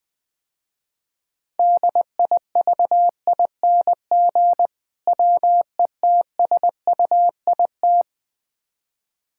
Morse code, heard as a keyed tone, about 20 wpm, 700 Hz. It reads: DIVING WETSUIT
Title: Difficult Words - Speed-Racing with Hints - 20wpm